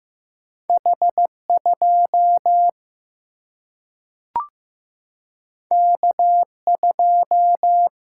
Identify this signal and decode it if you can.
Morse code, heard as a keyed tone, about 15 wpm, 700 Hz. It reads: H2 K2
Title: Single-Letter Single-Number 15wpm